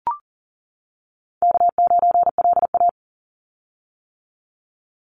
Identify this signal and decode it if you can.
Morse code, heard as a keyed tone, about 40 wpm, 700 Hz. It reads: K9LA
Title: Easy Call Signs - 40wpm